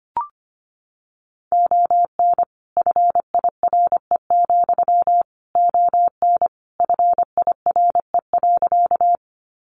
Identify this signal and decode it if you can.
Morse code, heard as a keyed tone, about 25 wpm, 700 Hz. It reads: ON FIRE, ON FIRE.
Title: Sentences from Top 400 Words 25wpm